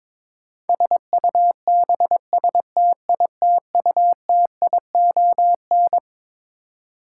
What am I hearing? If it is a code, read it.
Morse code, heard as a keyed tone, about 22 wpm, 700 Hz. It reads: SUBSTITUTION
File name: Twelve Letter Words - 22wpm